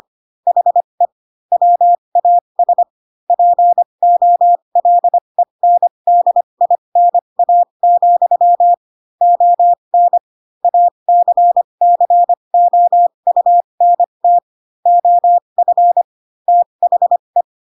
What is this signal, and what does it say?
Morse code, heard as a keyed tone, about 25 wpm, spaced as for 22 wpm, 700 Hz. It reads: HE WAS POLENDINA, ON ACCOUNT OF THE
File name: Pinocchio / Pinocchio ch2 18 25 ik1hge